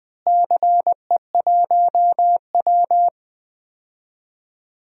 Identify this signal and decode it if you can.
Morse code, heard as a keyed tone, about 20 wpm, 700 Hz. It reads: CE1W
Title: DXCC-100 Call Signs - 4 Characters - Repeated 2x - 20wpm